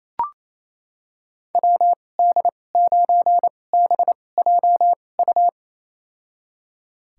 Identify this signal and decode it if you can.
Morse code, heard as a keyed tone, about 28 wpm, spaced as for 20 wpm, 700 Hz. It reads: WD9BJU
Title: USA Call Signs - 6 Characters - 2x Character Spacing - 28wpm